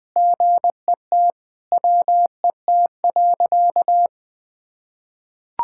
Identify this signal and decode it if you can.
Morse code, heard as a keyed tone, about 21 wpm, 700 Hz. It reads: GET WET.
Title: Sentences with 5-Word Phrases - Spoken Beginning - 20wpm